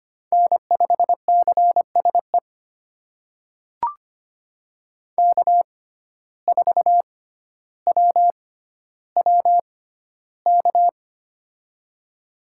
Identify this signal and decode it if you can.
Morse code, heard as a keyed tone, about 25 wpm, spaced as for 13 wpm, 700 Hz. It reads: N5CSE K4WWK
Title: USA Call Signs - 5 Characters - 6x Character Spacing - 25wpm